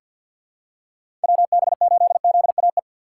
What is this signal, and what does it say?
Morse code, heard as a keyed tone, about 50 wpm, 700 Hz. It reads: WB8ZAE
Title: USA Call Signs - 2x Character Spacing - 50wpm